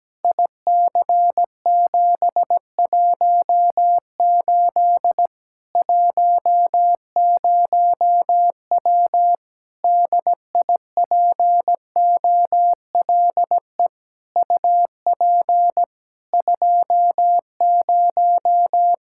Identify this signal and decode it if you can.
Morse code, heard as a keyed tone, about 17 wpm, 700 Hz. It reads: IC718 10W DIPOLE UP 20
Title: Practice QSO - Exchange 2 - Info - 17wpm